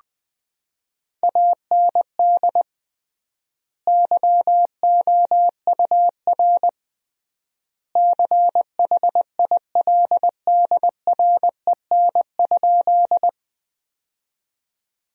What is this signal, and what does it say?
Morse code, heard as a keyed tone, about 21 wpm, 700 Hz. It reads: AND YOUR CHILDREN?
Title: 3x Word-Spacing - Sentences from Top 300 Words - 20wpm